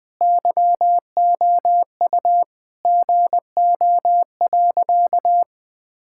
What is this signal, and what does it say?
Morse code, heard as a keyed tone, about 20 wpm, 700 Hz. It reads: YOU GO.
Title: Sentences from Top 1000 Words - Review Part 6 - 20wpm